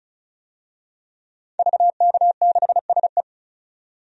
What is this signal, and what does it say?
Morse code, heard as a keyed tone, about 36 wpm, 700 Hz. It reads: VK6SE
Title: DXCC-100 Call Signs - 5 Characters - 35wpm